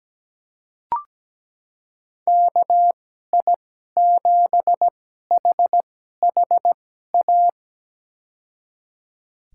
Morse code, heard as a keyed tone, about 17 wpm, spaced as for 12 wpm, 700 Hz. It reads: KI7HHA
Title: USA Call Signs - 2x Character Spacing - 17wpm